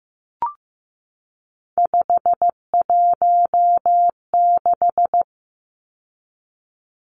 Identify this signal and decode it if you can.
Morse code, heard as a keyed tone, about 15 wpm, 700 Hz. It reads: 516
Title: Numbers - 3-Digit 15wpm